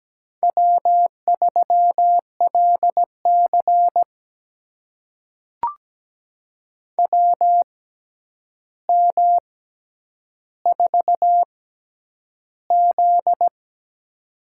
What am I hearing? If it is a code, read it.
Morse code, heard as a keyed tone, about 17 wpm, spaced as for 9 wpm, 700 Hz. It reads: W3LC WM4Z
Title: USA Call Signs - 4 Characters - 6x Character Spacing - 17wpm